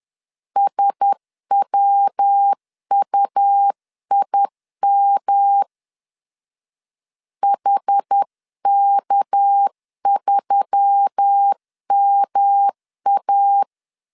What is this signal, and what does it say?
Morse code, heard as a keyed tone, about 10 wpm, 800 Hz. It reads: SWUIM HK3MA
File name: CW / DET / 1 6 1500C S C 10wpm 4252